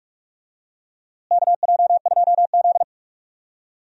Morse code, heard as a keyed tone, about 46 wpm, 700 Hz. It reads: KJ2Z
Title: USA Call Signs - 4 Characters - Repeated 2x - 45wpm